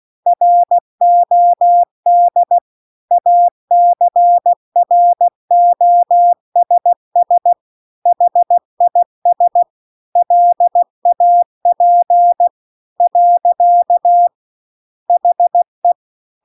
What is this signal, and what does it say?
Morse code, heard as a keyed tone, about 16 wpm, 700 Hz. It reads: ROD ACROSS HIS LAP . HE
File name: CW-The-Big-Twoheaded-River-2-16WPM